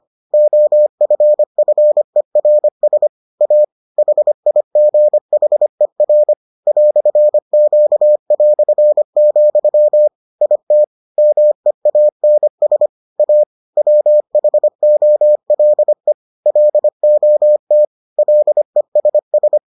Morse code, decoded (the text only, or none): OFFERS A HIGHER "Q", IT MEANS A WHOLE LOT LESS